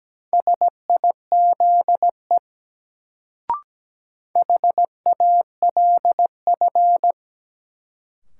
Morse code, heard as a keyed tone, about 17 wpm, 700 Hz. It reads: SIZE HALF
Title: Sentences from Top 400 Words - New Word Drill - 17wpm